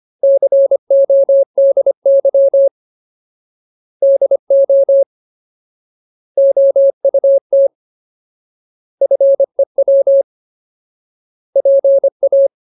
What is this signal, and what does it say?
Morse code, heard as a keyed tone, about 25 wpm, 550 Hz. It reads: CODY DO OUT FEW PA
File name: Ultimate Word List / 25WPM Ultimate